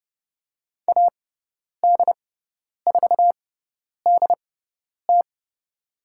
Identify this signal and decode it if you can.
Morse code, heard as a keyed tone, about 30 wpm, spaced as for 10 wpm, 700 Hz. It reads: AD4DT
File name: Lesson 18g - Letter m - Call Signs 30wpm